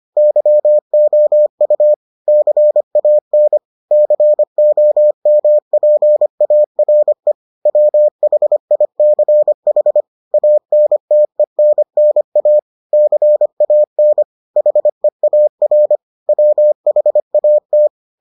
Morse code, heard as a keyed tone, about 25 wpm, 600 Hz. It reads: YOU CAN COMPARE WHICH ANTENNA CAN HEAR WHAT